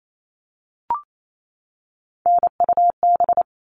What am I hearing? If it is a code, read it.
Morse code, heard as a keyed tone, about 28 wpm, 700 Hz. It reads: NUB
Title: All Cut Numbers - 3-Digit 28wpm